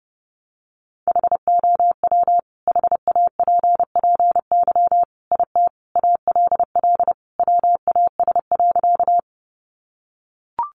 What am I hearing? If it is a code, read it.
Morse code, heard as a keyed tone, about 30 wpm, 700 Hz. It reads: HOW HAPPY IT ALL WAS.
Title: Sentences from Top 1000 Words 30wpm